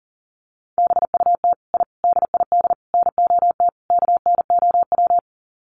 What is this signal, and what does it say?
Morse code, heard as a keyed tone, about 40 wpm, 700 Hz. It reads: BUT I DID NOT KNOW
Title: Sets of 5 Words Encore 40wpm